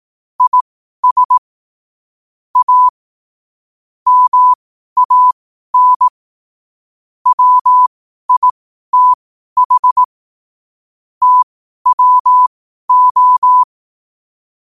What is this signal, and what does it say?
Morse code, heard as a keyed tone, about 18 wpm, spaced as for 12 wpm, 1000 Hz. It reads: IS A MAN WITH TWO
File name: qotdfiles / 12wpm